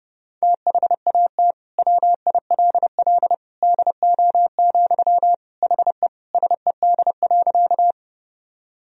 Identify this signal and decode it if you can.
Morse code, heard as a keyed tone, about 30 wpm, 700 Hz. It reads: THAT WILL DO, HE SED.
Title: Sentences from Top 100 Words and Abbreviations 30wpm